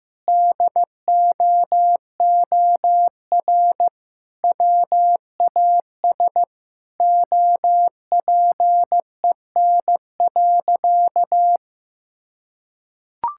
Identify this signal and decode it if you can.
Morse code, heard as a keyed tone, about 16 wpm, 700 Hz. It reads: DOOR WAS OPEN.
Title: Sentences from Top 1600 Words 15wpm